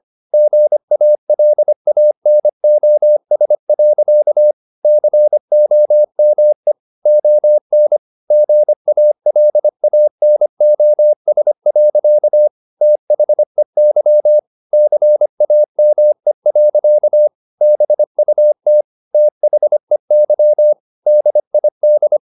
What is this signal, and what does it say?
Morse code, heard as a keyed tone, about 25 wpm, 600 Hz. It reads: GALANOS. COME ON GALANOS. THEY CAME. BUT THEY DID